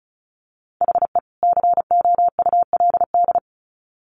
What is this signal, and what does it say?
Morse code, heard as a keyed tone, about 34 wpm, 700 Hz. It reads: HE COULD
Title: Sets of 2 Words 35wpm